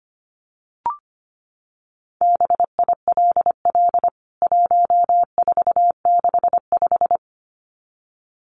Morse code, heard as a keyed tone, about 25 wpm, 700 Hz. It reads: BILL 1465
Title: CWT Exchange 25wpm